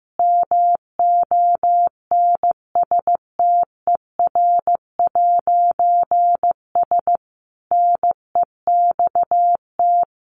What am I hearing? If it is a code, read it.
Morse code, heard as a keyed tone, about 15 wpm, 700 Hz. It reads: MONSTER'S NEXT
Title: A Sound Of Thunder 15wpm